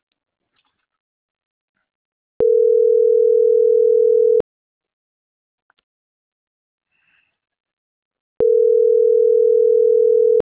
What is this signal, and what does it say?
Ringback tone (North America), 440 + 480 Hz.